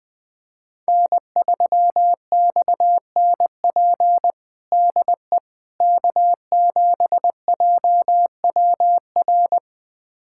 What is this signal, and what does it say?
Morse code, heard as a keyed tone, about 20 wpm, 700 Hz. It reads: N3XNP DE K7JWR
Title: USA Callsign DE Callsign - 5 Characters - 20wpm